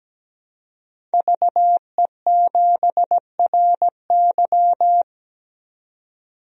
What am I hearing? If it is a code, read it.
Morse code, heard as a keyed tone, about 17 wpm, 700 Hz. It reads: VE7RY
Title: DXCC-100 Call Signs - 5 Characters - 17wpm